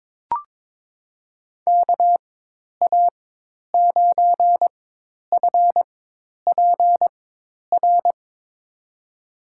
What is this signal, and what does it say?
Morse code, heard as a keyed tone, about 22 wpm, spaced as for 10 wpm, 700 Hz. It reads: KA9FPR
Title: USA Call Signs - 6 Characters - 4x Character Spacing - 22wpm